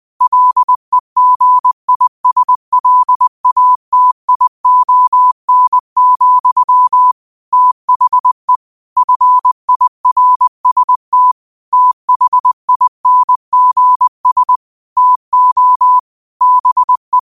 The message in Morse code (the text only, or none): LEGISLATION, THE FIRST THINGS TO BE